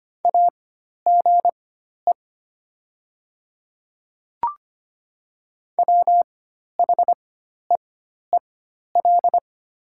Morse code, heard as a keyed tone, about 25 wpm, spaced as for 12 wpm, 700 Hz. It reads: AGE WHEEL
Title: 4x Character Spacing - Top 500 Words - 25wpm